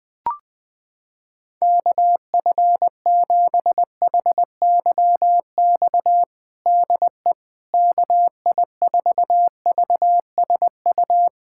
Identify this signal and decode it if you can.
Morse code, heard as a keyed tone, about 20 wpm, 700 Hz. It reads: KF7HYX DE KI4VSU
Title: USA Callsign DE Callsign - 6 Characters - 20wpm